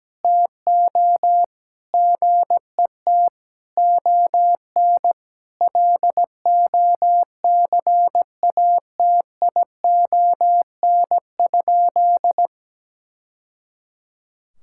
Morse code, heard as a keyed tone, about 17 wpm, 700 Hz. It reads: TO GET ON LOCATION?
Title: Sentences from Top 1000 Words - Review Part 6 - 17wpm